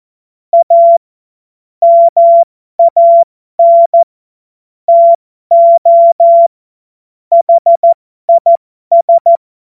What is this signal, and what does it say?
Morse code, heard as a keyed tone, about 14 wpm, spaced as for 12 wpm, 680 Hz. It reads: A MAN TO HIS
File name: Morsekurs / Witze